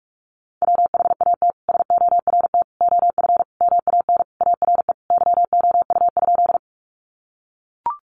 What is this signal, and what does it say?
Morse code, heard as a keyed tone, about 44 wpm, 700 Hz. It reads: WHAT SORT OF MAN ARE YOU?
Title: Sentences from Top 1300 Words 45wpm